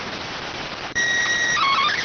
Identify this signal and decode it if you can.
SSTV (slow-scan television), PD 290.